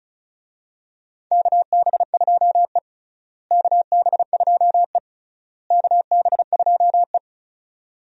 Morse code, heard as a keyed tone, about 36 wpm, 700 Hz. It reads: KB2E KB2E KB2E
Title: USA Call Signs - 4 Characters - Repeated 3x - 35wpm